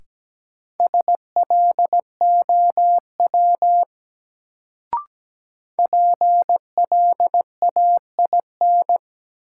Morse code, heard as a keyed tone, about 17 wpm, 700 Hz. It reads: SLOW PLAIN
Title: Sentences from Top 400 Words - New Word Drill - 17wpm